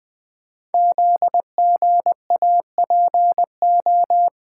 Morse code, heard as a keyed tone, about 20 wpm, 700 Hz. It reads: ZGAPO
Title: Code Groups - 5 Letters - 20wpm